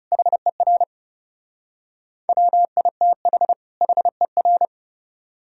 Morse code, heard as a keyed tone, about 35 wpm, 700 Hz. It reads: HER WITH HER
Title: Speed-Racing - Sets of 2 Words 30wpm